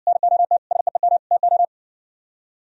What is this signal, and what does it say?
Morse code, heard as a keyed tone, about 62 wpm, 700 Hz. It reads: NOT SEEM TO